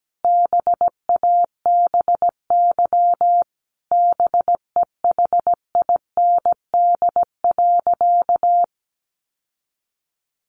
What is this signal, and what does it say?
Morse code, heard as a keyed tone, about 17 wpm, 700 Hz. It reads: BABY BEHIND.
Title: Sentences from Top 1300 Words 17wpm